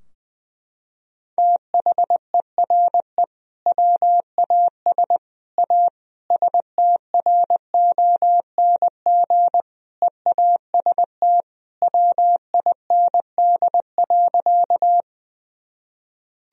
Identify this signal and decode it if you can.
Morse code, heard as a keyed tone, about 20 wpm, 700 Hz. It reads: THERE WAS A STRONG EAST WIND.
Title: Sentences from Top 1000 Words - Review Part 3 - 20wpm